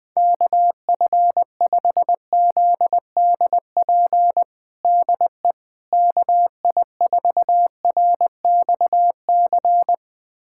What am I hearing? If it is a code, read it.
Morse code, heard as a keyed tone, about 20 wpm, 700 Hz. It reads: KF5ZDP DE KI4RXC
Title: USA Callsign DE Callsign 20wpm